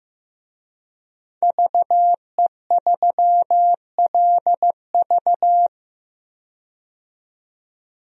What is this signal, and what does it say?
Morse code, heard as a keyed tone, about 15 wpm, 700 Hz. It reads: VE3LV
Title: DXCC-100 Call Signs - 5 Characters - 15wpm